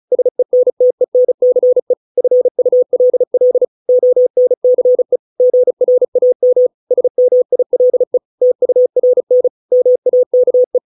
Morse code, heard as a keyed tone, about 34 wpm, 500 Hz. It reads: SENTENCE FULL ONCE GRAM SMILE TURN MAKE